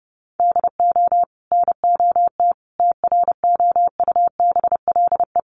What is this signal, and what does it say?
Morse code, heard as a keyed tone, about 30 wpm, 700 Hz. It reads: DO NOT TROUBLE
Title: Sentences from Top 1200 Words 30wpm